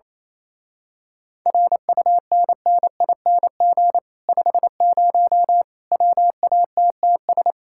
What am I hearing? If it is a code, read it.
Morse code, heard as a keyed tone, about 26 wpm, 700 Hz. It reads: RUNNING 50 WATTS
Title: Speed-Racing Common QSO Phrases Encore - 28wpm